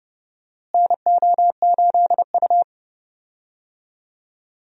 Morse code, heard as a keyed tone, about 30 wpm, 700 Hz. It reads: NO8U
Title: USA Call Signs - 4 Characters - Repeated 3x - 30wpm